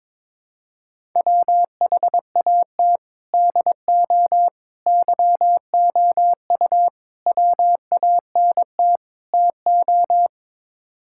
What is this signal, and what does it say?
Morse code, heard as a keyed tone, about 22 wpm, 700 Hz. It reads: WHAT DO YOU WANT TO
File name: practice / 22wpm / Build A Phrase - 5-Word Phrases - 22wpm